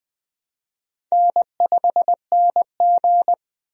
Morse code, heard as a keyed tone, about 20 wpm, 700 Hz. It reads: N5NG
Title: USA Call Signs - 4 Characters - 20wpm